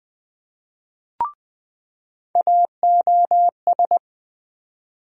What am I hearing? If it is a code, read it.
Morse code, heard as a keyed tone, about 20 wpm, 700 Hz. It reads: AOS
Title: USA Call Sign Postfixes - 3 Letters 20wpm